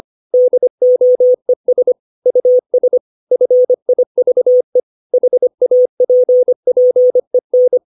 Morse code, heard as a keyed tone, about 25 wpm, 500 Hz. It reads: DOES US FIVE HAPPEN
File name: 632 words / 632 Words 25WPM